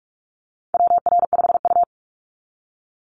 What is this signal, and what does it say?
Morse code, heard as a keyed tone, about 44 wpm, 700 Hz. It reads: WR5U
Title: Call Signs Repeated 2x - 45wpm